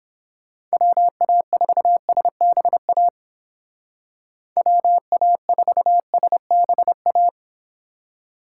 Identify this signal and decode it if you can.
Morse code, heard as a keyed tone, about 29 wpm, 700 Hz. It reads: WA4SBA WA4SBA